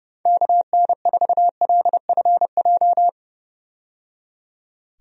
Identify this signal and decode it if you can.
Morse code, heard as a keyed tone, about 30 wpm, 700 Hz. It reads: KN4LFJ
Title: USA Call Signs - 6 Characters - 30wpm